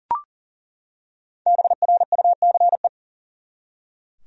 Morse code, heard as a keyed tone, about 39 wpm, 700 Hz. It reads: BRUCE
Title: USA Common Names - Max 5 Letters - 40wpm